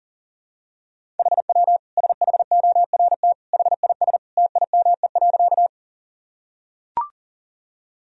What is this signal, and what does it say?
Morse code, heard as a keyed tone, about 37 wpm, 700 Hz. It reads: HW SHORT HIS TIME.
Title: Sentences from Top 400 Words and Abbreviations 40wpm